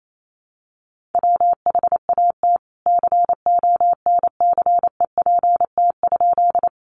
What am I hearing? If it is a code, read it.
Morse code, heard as a keyed tone, about 28 wpm, 700 Hz. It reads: WHAT CONCEPT?
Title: Sentences from Top 1100 Words 28wpm